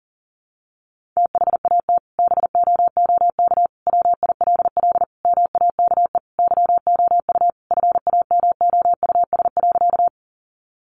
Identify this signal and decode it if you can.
Morse code, heard as a keyed tone, about 40 wpm, 700 Hz. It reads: THAT BOOK WILL MAKE YOU FAMOUS.
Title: Sentences from Top 1500 Words 40wpm